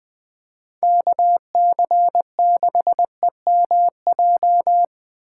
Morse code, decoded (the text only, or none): KC6EMJ